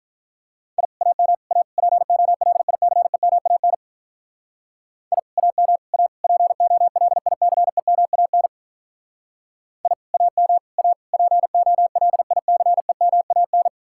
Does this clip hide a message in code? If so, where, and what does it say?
Yes: Morse code from 0.8 to 13.7 s, heard as a keyed tone, about 45 wpm, 700 Hz. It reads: I AM A POLICEMAN I AM A POLICEMAN I AM A POLICEMAN